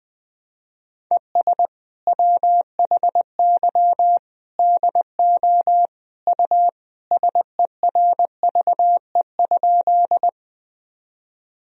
Morse code, heard as a keyed tone, about 20 wpm, 700 Hz. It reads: ES WHY DO U SERVE?
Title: Sentences from Top 400 Words and Abbreviations 20wpm